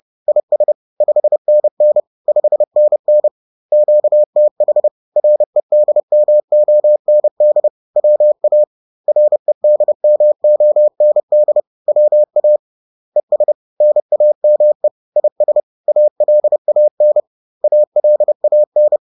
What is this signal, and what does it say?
Morse code, heard as a keyed tone, about 30 wpm, 600 Hz. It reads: IS 5NN 5NN QTH REDMOND WA REDMOND WA ES NAME IS ALAN ALAN
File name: 30wpm